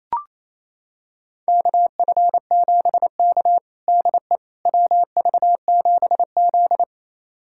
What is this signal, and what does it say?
Morse code, heard as a keyed tone, about 28 wpm, 700 Hz. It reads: KF7K DE WV7Z
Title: USA Callsign DE Callsign - 4 Characters - 28wpm